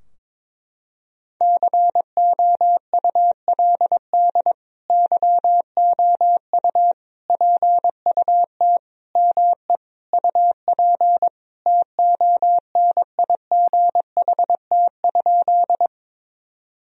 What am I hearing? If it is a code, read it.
Morse code, heard as a keyed tone, about 22 wpm, 700 Hz. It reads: COULD YOU PUT ME UP TONIGHT?
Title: Sentences from Top 1000 Words - Review Part 3 - 22wpm